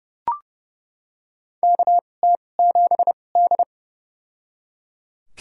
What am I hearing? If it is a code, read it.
Morse code, heard as a keyed tone, about 30 wpm, spaced as for 22 wpm, 700 Hz. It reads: KT7D
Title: USA Call Signs - 4 Characters - 2x Character Spacing - 30wpm